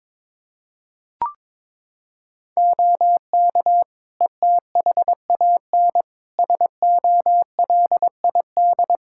Morse code, heard as a keyed tone, about 22 wpm, 700 Hz. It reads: OK ETHAN SOLID
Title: Practice QSO - Exchange 3 - More Info - 22wpm